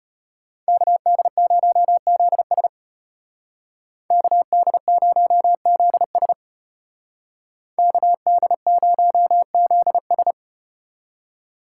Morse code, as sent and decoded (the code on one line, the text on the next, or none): -.- -.. ----- --.. ... / -.- -.. ----- --.. ... / -.- -.. ----- --.. ...
KD0ZS KD0ZS KD0ZS